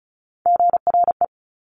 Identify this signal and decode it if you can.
Morse code, heard as a keyed tone, about 36 wpm, 700 Hz. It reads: GRE
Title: Top Three Letter Combinations in Words 35wpm